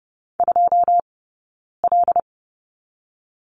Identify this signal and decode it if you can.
Morse code, heard as a keyed tone, about 30 wpm, 700 Hz. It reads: 2 L
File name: Lesson 20d - Number 3 - Two Characters 30wpm